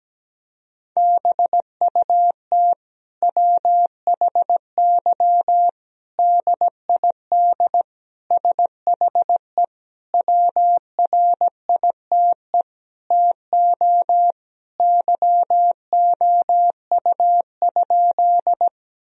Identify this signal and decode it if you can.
Morse code, heard as a keyed tone, about 17 wpm, 700 Hz. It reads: BUT WHY DID SHE WRITE TO YOU?
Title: Sentences from Top 1000 Words - Review Part 1 - 17wpm